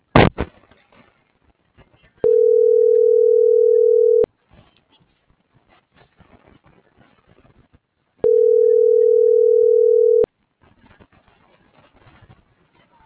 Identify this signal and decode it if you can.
Ringback tone (North America), 440 + 480 Hz.